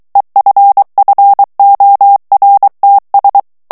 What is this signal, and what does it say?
Morse code, heard as a keyed tone, about 23 wpm, 800 Hz. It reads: EFFORTS
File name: morse mp3 / FRANCAIS TEXTE CLAIR 1 VITESSE 22 MOTS MINUTES